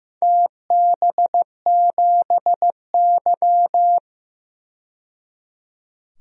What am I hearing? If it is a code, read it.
Morse code, heard as a keyed tone, about 15 wpm, 700 Hz. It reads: TB7Y
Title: DXCC-100 Call Signs - 4 Characters - 15wpm